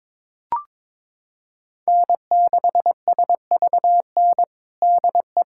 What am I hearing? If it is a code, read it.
Morse code, heard as a keyed tone, about 22 wpm, 700 Hz. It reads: N6SVN DE
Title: USA Callsign DE Callsign - 5 Characters - 22wpm